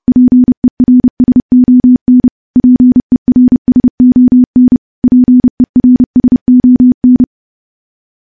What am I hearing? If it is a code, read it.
Morse code, heard as a keyed tone, about 30 wpm, 250 Hz. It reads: PERSON PERSON PERSON
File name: morse / morse-training-30WPM